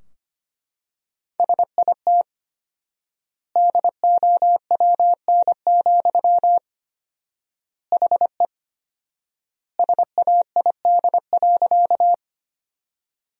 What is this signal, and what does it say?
Morse code, heard as a keyed tone, about 25 wpm, 700 Hz. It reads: SIT DOWN, HE SAID.
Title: 4x Word-Spacing - Sentences from Top 500 Words - 25wpm